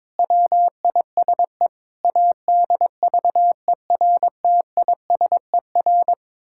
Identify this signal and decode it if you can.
Morse code, heard as a keyed tone, about 22 wpm, 700 Hz. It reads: WISE ADVERTISER
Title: Pangram Challenge v2 - 22wpm